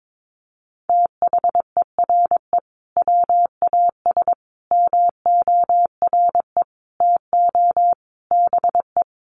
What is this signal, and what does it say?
Morse code, heard as a keyed tone, about 22 wpm, 700 Hz. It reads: THERE WAS MORE TO BE
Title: Sentences from Top 500 Words 22wpm